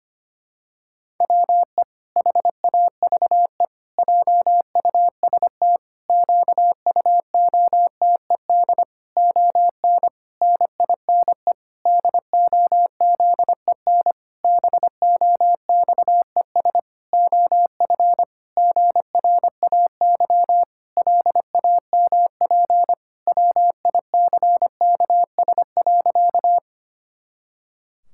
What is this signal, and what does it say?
Morse code, heard as a keyed tone, about 25 wpm, 700 Hz. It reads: WE HAVE JUST QUOTED ON NINE DOZEN BOXES OF GRAY LAMP WICKS.
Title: Pangram Challenge v2 - 25wpm